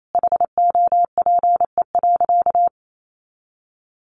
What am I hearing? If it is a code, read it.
Morse code, heard as a keyed tone, about 28 wpm, 700 Hz. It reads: HOPE.
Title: Sentences from Top 1000 Words 28wpm